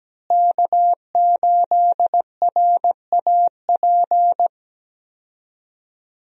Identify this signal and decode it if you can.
Morse code, heard as a keyed tone, about 17 wpm, 700 Hz. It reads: K8RAP
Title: USA Call Signs - 5 Characters - Repeated 3x - 17wpm